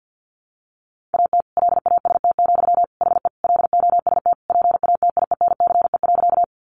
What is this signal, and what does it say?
Morse code, heard as a keyed tone, about 50 wpm, 700 Hz. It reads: AT LAST, HE LOST PATIENCE.
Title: Sentences from Top 1400 Words 50wpm